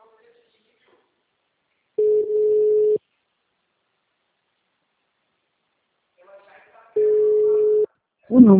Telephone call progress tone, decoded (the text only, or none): ringback tone